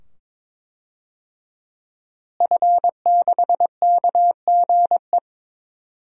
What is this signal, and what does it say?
Morse code, heard as a keyed tone, about 22 wpm, 700 Hz. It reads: F6KGE
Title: DXCC-100 Call Signs - 5 Characters - Repeated 2x - 22wpm